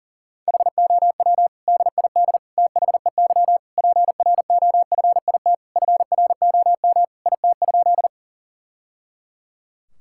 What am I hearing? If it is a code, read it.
Morse code, heard as a keyed tone, about 37 wpm, 700 Hz. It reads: HOW DID THEY PROFIT FROM IT?
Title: Sentences from Top 1000 Words - Review Part 5 - 40wpm